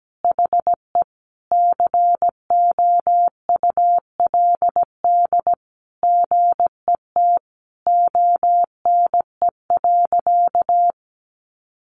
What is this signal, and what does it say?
Morse code, heard as a keyed tone, about 17 wpm, 700 Hz. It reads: HE COULD GET ONE.
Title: Sentences from Top 1400 Words 17wpm